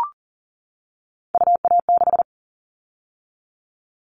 Morse code, heard as a keyed tone, about 40 wpm, 700 Hz. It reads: UA6
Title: Callsign Prefixes for Top 50 DXCC - 40wpm